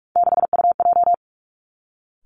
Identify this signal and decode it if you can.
Morse code, heard as a keyed tone, about 46 wpm, 700 Hz. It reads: 6UJ
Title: Three Challenging Letter-Number Combinations 45wpm